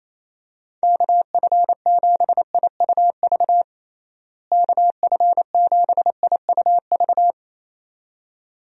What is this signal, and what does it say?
Morse code, heard as a keyed tone, about 28 wpm, 700 Hz. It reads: KF7IUV KF7IUV
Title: USA Call Signs - 6 Characters - Repeated 2x - 28wpm